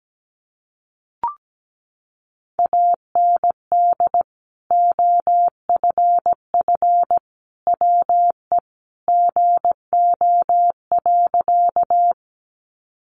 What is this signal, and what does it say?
Morse code, heard as a keyed tone, about 17 wpm, 700 Hz. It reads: AND OFF WE GO.
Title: Sentences from Top 200 Words 17wpm